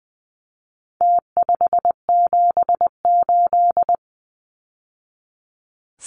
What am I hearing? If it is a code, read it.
Morse code, heard as a keyed tone, about 20 wpm, 700 Hz. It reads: T578
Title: Numbers - Contest Style - 4-Digit 20wpm